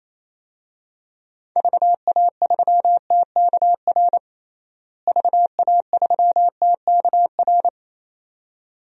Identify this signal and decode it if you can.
Morse code, heard as a keyed tone, about 27 wpm, 700 Hz. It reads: VA3TKR VA3TKR
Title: DXCC-100 Call Signs - 6 Characters - Repeated 2x - 28wpm